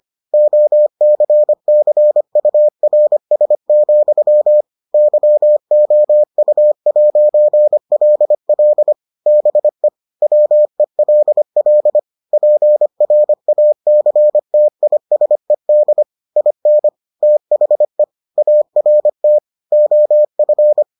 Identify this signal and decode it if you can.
Morse code, heard as a keyed tone, about 25 wpm, 600 Hz. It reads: OCCURS, YOU'LL BE WELL PRACTISED IN THE ART OF